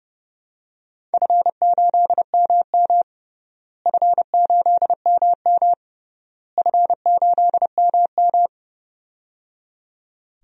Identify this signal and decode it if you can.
Morse code, heard as a keyed tone, about 30 wpm, 700 Hz. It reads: F8MM F8MM F8MM
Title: DXCC-100 Call Signs - 4 Characters - Repeated 3x - 30wpm